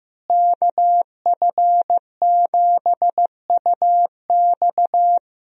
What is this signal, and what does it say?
Morse code, heard as a keyed tone, about 15 wpm, 700 Hz. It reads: KF7UX
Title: USA Call Signs - 5 Characters - Repeated 3x - 15wpm